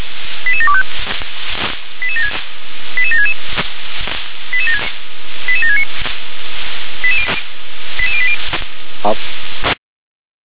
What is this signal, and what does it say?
Five-tone selective call (ZVEI), heard as quick runs of five tones, tones of about 70 ms. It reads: 88725 88766 88769 88080